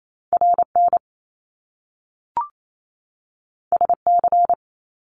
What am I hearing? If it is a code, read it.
Morse code, heard as a keyed tone, about 28 wpm, 700 Hz. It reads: RN SC
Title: Top Two Letter Combinations in Words 28wpm